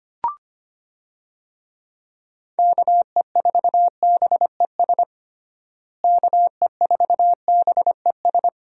Morse code, heard as a keyed tone, about 25 wpm, 700 Hz. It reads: KE4BES KE4BES
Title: USA Call Signs - 6 Characters - Repeated 2x - 25wpm